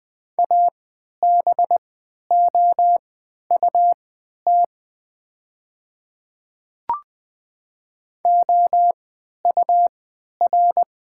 Morse code, heard as a keyed tone, about 20 wpm, spaced as for 11 wpm, 700 Hz. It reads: ABOUT OUR
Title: 3x Character Spacing - Top 300 Words - 20wpm